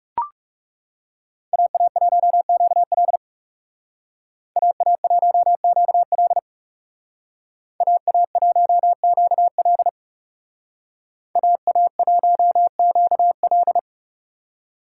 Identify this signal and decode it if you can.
Morse code, heard as a keyed tone, about 36 wpm, 700 Hz. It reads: AA1QL AA1QL AA1QL AA1QL
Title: Speed Racing - USA 2x2 Call Signs 30wpm